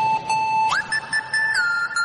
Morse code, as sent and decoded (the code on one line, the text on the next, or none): none